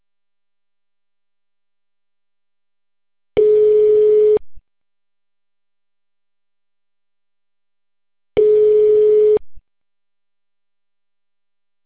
Ringback tone (Europe, 425 Hz).